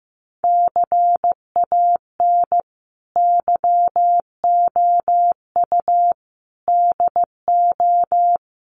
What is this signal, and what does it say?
Morse code, heard as a keyed tone, about 15 wpm, 700 Hz. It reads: CAN YOU DO
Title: Sentences from Top 100 Words 15wpm